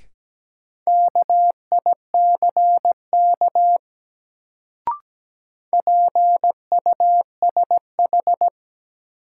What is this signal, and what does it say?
Morse code, heard as a keyed tone, about 17 wpm, 700 Hz. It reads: KICK PUSH
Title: Sentences from Top 1300 Words - New Word Drill Alt - 17wpm